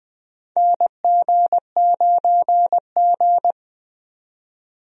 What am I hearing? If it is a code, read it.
Morse code, heard as a keyed tone, about 20 wpm, 700 Hz. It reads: NG9G
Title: USA Call Signs - 4 Characters - 20wpm